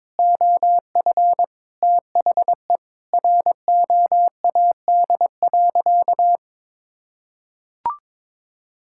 Morse code, heard as a keyed tone, about 22 wpm, 700 Hz. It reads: OF THE ROAD.
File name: practice / 22wpm / Sentences with 3-Word Phrases - 22wpm